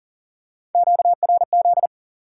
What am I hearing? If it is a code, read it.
Morse code, heard as a keyed tone, about 40 wpm, 700 Hz. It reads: QRZ
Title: Top 90 QSO Elements 40wpm